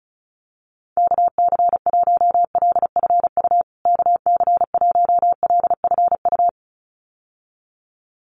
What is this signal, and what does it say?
Morse code, heard as a keyed tone, about 34 wpm, 700 Hz. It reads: KC1LFU KC1LFU
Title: USA Call Signs Repeated 2x 35wpm